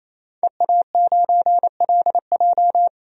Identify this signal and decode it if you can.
Morse code, heard as a keyed tone, about 28 wpm, 700 Hz. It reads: EA9LJ
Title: DXCC-100 Call Signs - 5 Characters - 28wpm